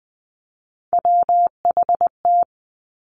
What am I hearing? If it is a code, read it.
Morse code, heard as a keyed tone, about 20 wpm, 700 Hz. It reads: WHT
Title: Three Challenging Letter-Number Combinations 20wpm